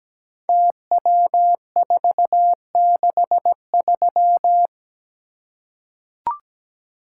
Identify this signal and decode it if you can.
Morse code, heard as a keyed tone, about 17 wpm, 700 Hz. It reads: TW463
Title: Code Groups - 5 Letter:Numbers - 17wpm